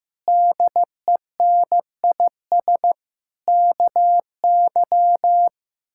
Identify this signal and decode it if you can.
Morse code, heard as a keyed tone, about 15 wpm, 700 Hz. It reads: DENIS KY
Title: K1USN SST QSO Practice - 15wpm